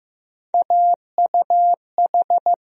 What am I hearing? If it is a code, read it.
Morse code, heard as a keyed tone, about 15 wpm, 700 Hz. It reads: AUH
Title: USA Call Sign Postfixes - 3 Letters 15wpm